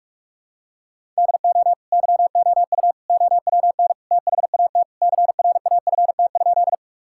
Morse code, heard as a keyed tone, about 41 wpm, 700 Hz. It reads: DO YOU OWN THAT CRAFT?
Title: Sentences from Top 1000 Words - Review Part 6 - 45wpm